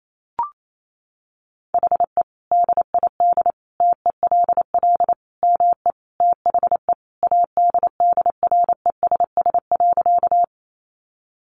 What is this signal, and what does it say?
Morse code, heard as a keyed tone, about 28 wpm, 700 Hz. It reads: HE DID TELL ME THE ADDRESS.
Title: Sentences from Top 1000 Words 28wpm